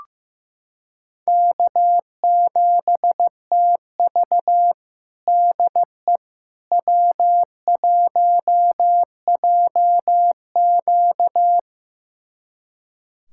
Morse code, heard as a keyed tone, about 15 wpm, 700 Hz. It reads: K7TV DE W1JQ
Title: USA Callsign DE Callsign - 4 Characters - 15wpm